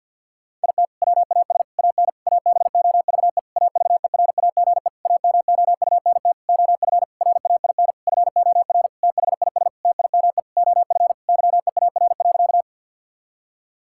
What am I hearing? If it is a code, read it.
Morse code, heard as a keyed tone, about 52 wpm, 700 Hz. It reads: IT WAS AN ABOVE AVERAGE AMOUNT OF RAIN FOR THIS TIME OF YEAR.
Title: Sentences from Top 1000 Words - Review Part 6 - 50wpm